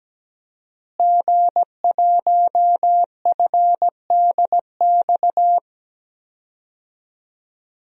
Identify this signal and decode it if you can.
Morse code, heard as a keyed tone, about 17 wpm, 700 Hz. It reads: G1FDX
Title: DXCC-100 Call Signs - 5 Characters - Repeated 3x - 17wpm